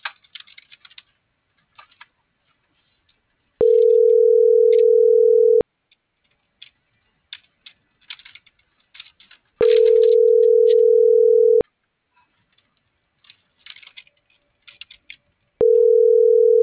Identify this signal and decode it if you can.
Ringback tone (North America), 440 + 480 Hz.